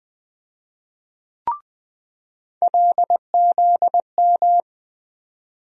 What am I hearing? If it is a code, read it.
Morse code, heard as a keyed tone, about 20 wpm, 700 Hz. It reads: LZM
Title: USA Call Sign Postfixes - 3 Letters 20wpm